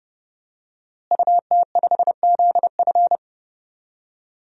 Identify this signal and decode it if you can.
Morse code, heard as a keyed tone, about 30 wpm, 700 Hz. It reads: UT5ZF
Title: DXCC-100 Call Signs - 5 Characters - Repeated 3x - 30wpm